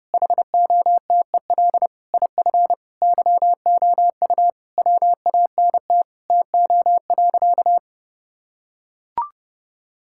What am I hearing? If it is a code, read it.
Morse code, heard as a keyed tone, about 30 wpm, 700 Hz. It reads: HOTEL IF YOU WANT TO.
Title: Sentences from Top 1600 Words 30wpm